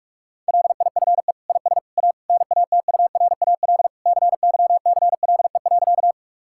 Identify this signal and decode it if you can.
Morse code, heard as a keyed tone, about 41 wpm, 700 Hz. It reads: LIFE IS A NATURAL CYCLE.
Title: Sentences from Top 1000 Words - Review Part 6 - 45wpm